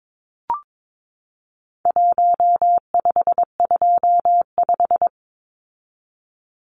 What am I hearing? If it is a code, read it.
Morse code, heard as a keyed tone, about 22 wpm, 700 Hz. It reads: 1525
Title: Numbers - Contest Style - 4-Digit 22wpm